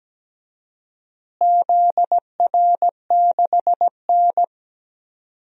Morse code, heard as a keyed tone, about 17 wpm, 700 Hz. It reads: ZR6N
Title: DXCC-100 Call Signs - 4 Characters - Repeated 3x - 17wpm